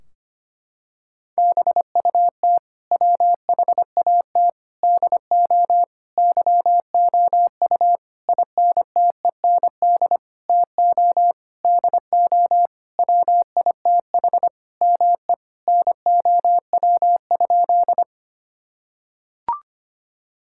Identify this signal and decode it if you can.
Morse code, heard as a keyed tone, about 25 wpm, 700 Hz. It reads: BUT WHAT DO YOU INTEND TO DO WITH ME NOW?
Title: Sentences from Top 1600 Words 25wpm